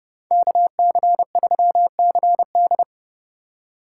Morse code, heard as a keyed tone, about 30 wpm, 700 Hz. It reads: KC3CD